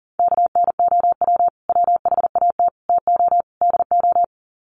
Morse code, heard as a keyed tone, about 38 wpm, 700 Hz. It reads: KNOW WHAT TO DO
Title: Sets of 4 Words 40wpm